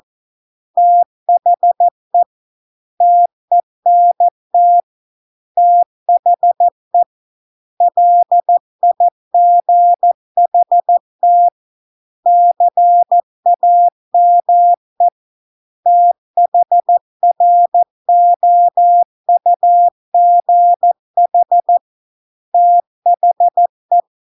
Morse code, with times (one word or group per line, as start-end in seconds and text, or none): 0.8-2.2 THE
3.0-4.8 TENT
5.6-7.0 THE
7.8-11.5 LIGHT
12.3-15.1 CAME
15.9-21.8 THROUGH
22.5-24.0 THE